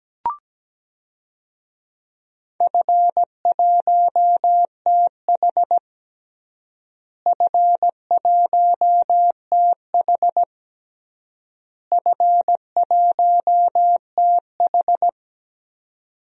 Morse code, heard as a keyed tone, about 17 wpm, 700 Hz. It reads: F1TH F1TH F1TH
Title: DXCC-100 Call Signs - 4 Characters - Repeated 3x - 17wpm